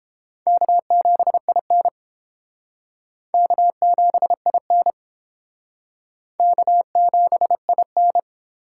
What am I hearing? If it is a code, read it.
Morse code, heard as a keyed tone, about 31 wpm, 700 Hz. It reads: K7IN K7IN K7IN